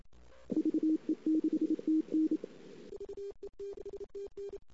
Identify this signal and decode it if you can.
Morse code, heard as a keyed tone, about 27 wpm, 310 Hz. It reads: VE6TN
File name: CW / 2008-ss-cw-k5tr-op-n5rz -19:00